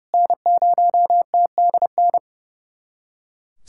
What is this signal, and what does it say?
Morse code, heard as a keyed tone, about 30 wpm, 700 Hz. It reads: N0TDN
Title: SOTA QSO Practice 30wpm